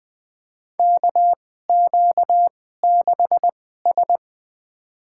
Morse code, heard as a keyed tone, about 20 wpm, spaced as for 15 wpm, 700 Hz. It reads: KQ6S
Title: USA Call Signs - 4 Characters - 2x Character Spacing - 20wpm